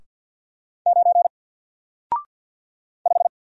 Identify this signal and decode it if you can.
Morse code, heard as a keyed tone, about 48 wpm, 700 Hz. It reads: 9 5
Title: Single Character (Letter:Number:Punctuation:Prosign) Alt 50wpm